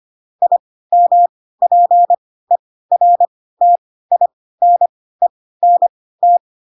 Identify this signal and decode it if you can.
Morse code, heard as a keyed tone, about 25 wpm, spaced as for 16 wpm, 700 Hz. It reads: IMPERTINENT
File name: Pinocchio / Pinocchio ch3 10 25 ik1hge